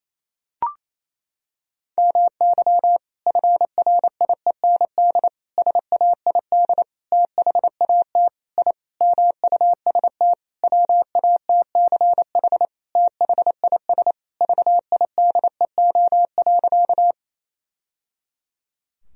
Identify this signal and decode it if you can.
Morse code, heard as a keyed tone, about 28 wpm, 700 Hz. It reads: MY FRIEND SAID THAT I MUST WATCH THIS VIDEO.
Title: Sentences from Top 1000 Words - Review Part 4 - 28wpm